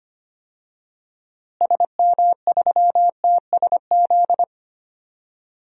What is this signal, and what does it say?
Morse code, heard as a keyed tone, about 25 wpm, 700 Hz. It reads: SM3TSZ
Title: DXCC-100 Call Signs - 6 Characters - Repeated 2x - 25wpm